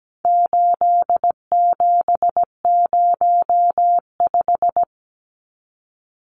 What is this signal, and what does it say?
Morse code, heard as a keyed tone, about 17 wpm, 700 Hz. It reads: 8705
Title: Numbers - 4-Digit 17wpm